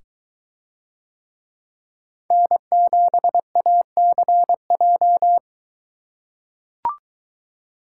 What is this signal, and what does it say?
Morse code, heard as a keyed tone, about 22 wpm, 700 Hz. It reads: N7ACJ